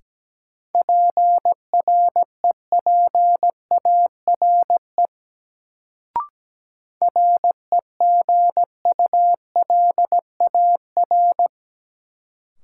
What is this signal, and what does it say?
Morse code, heard as a keyed tone, about 17 wpm, 700 Hz. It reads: PREPARE REGULAR
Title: Sentences from Top 900 Words - New Word Drill - 17wpm